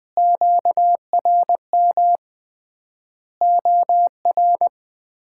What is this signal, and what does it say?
Morse code, heard as a keyed tone, about 20 wpm, 700 Hz. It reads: QRM OR
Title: Common QSO Phrases - 3x Word-Spacing - 20wpm